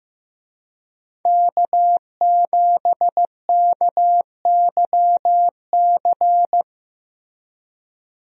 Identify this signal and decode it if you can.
Morse code, heard as a keyed tone, about 15 wpm, 700 Hz. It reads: K7KYC
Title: USA Call Signs - 5 Characters - Repeated 2x - 15wpm